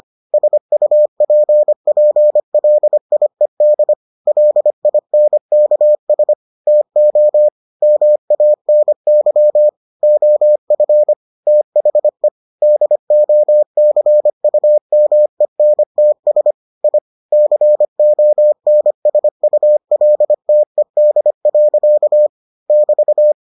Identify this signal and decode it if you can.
Morse code, heard as a keyed tone, about 25 wpm, 600 Hz. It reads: SUPPLIED LINKS TO MANY OF THE DOCUMENTS I CONSULTED. <BT>